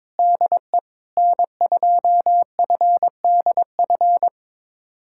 Morse code, heard as a keyed tone, about 22 wpm, 700 Hz. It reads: DE N2FDF
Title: USA Callsign DE Callsign 22wpm